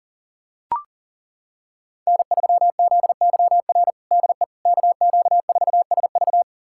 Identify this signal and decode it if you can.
Morse code, heard as a keyed tone, about 40 wpm, 700 Hz. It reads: N3ZYR DE KQ4SV
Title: USA Callsign DE Callsign - 5 Characters - 40wpm